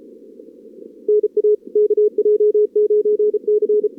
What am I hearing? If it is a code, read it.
Morse code, heard as a keyed tone, about 34 wpm, 440 Hz. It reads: NA KJ9C
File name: CWSprint / Feb2022 / k5tr-feb-na-sprint-cw-k5ot -0100